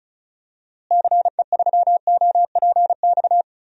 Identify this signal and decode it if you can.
Morse code, heard as a keyed tone, about 36 wpm, 700 Hz. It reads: CE3OPX
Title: DXCC-100 Call Signs - 6 Characters - Repeated 2x - 35wpm